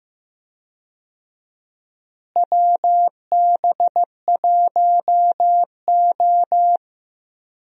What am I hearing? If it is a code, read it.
Morse code, heard as a keyed tone, about 15 wpm, 700 Hz. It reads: WB1O
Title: USA Call Signs - 4 Characters - Repeated 3x - 15wpm